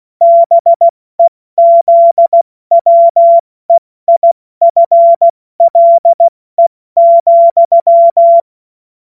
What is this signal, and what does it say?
Morse code, heard as a keyed tone, about 16 wpm, 680 Hz. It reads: BEZWEIFLE,
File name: Morsekurs / Witze / Witz 14.November 2023 70 F80